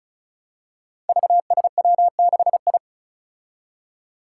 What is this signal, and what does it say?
Morse code, heard as a keyed tone, about 36 wpm, 700 Hz. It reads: VSW6I
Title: Five Challenging Letter-Number Combinations 35wpm